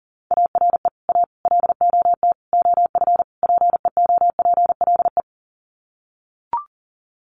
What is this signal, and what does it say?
Morse code, heard as a keyed tone, about 40 wpm, 700 Hz. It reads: ARE A LOT OF PEOPLE
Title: Sets of 5 Words 40wpm